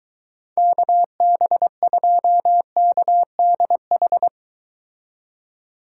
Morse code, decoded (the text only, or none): KB2KDH